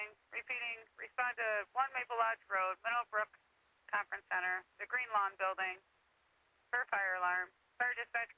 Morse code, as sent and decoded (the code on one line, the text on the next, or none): none